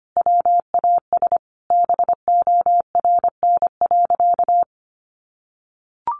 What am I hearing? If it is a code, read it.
Morse code, heard as a keyed tone, about 25 wpm, 700 Hz. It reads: WAS BORN.
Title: Sentences from Top 1300 Words 25wpm